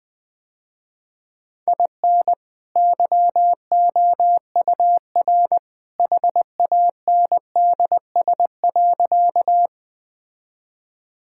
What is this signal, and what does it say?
Morse code, heard as a keyed tone, about 20 wpm, 700 Hz. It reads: IN YOUR HANDS.
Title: Sentences with 2-Word Phrases - Spoken Beginning - 20wpm